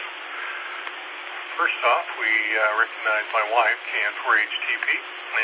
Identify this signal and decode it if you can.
Emergency Alert System (SAME) tones, heard as harsh buzzing bursts, with speech. It is heard from 0.3 to 4.5 s.